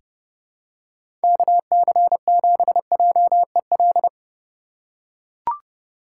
Morse code, heard as a keyed tone, about 30 wpm, 700 Hz. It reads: KC7JEL
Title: USA Call Signs - 3x Character Spacing - 30wpm